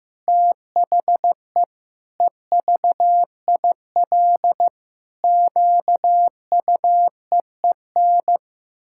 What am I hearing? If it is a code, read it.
Morse code, heard as a keyed tone, about 15 wpm, 700 Hz. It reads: THE EVIL QUEEN
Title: Pangram Challenge v2 - 15wpm